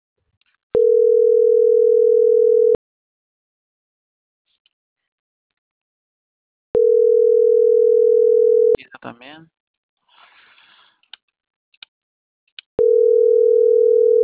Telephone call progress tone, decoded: ringback tone

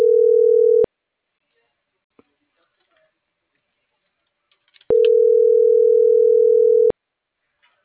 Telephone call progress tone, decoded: ringback tone

